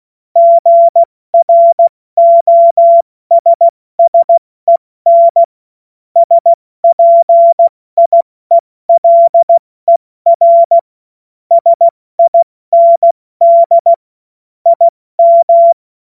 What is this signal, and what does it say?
Morse code, heard as a keyed tone, about 16 wpm, 680 Hz. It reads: GROSSEN SPIELER SIND IM
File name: Morsekurs / Witze / Witz 02. Februar 2021, 70 F80